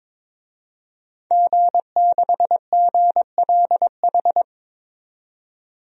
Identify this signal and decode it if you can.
Morse code, heard as a keyed tone, about 22 wpm, 700 Hz. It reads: G6GLH
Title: DXCC-100 Call Signs - 5 Characters - 22wpm